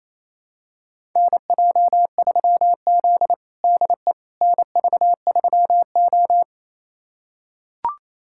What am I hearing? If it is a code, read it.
Morse code, heard as a keyed tone, about 28 wpm, 700 Hz. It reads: NJ3Z DE NV3O
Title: USA Callsign DE Callsign - 4 Characters - 28wpm